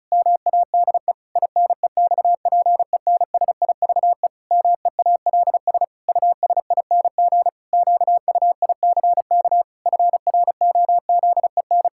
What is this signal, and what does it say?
Morse code, heard as a keyed tone, about 36 wpm, 700 Hz. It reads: MADE INEXPENSIVE MEALS USING QUICK FROZEN
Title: Pangram Challenge v2 - 35wpm